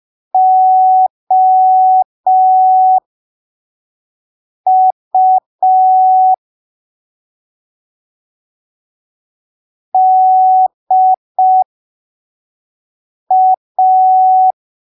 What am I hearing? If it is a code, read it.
Morse code, heard as a keyed tone, about 5 wpm, spaced as for 3 wpm, 750 Hz. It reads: OU DA
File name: arquivos / texto / texto 04 5ppm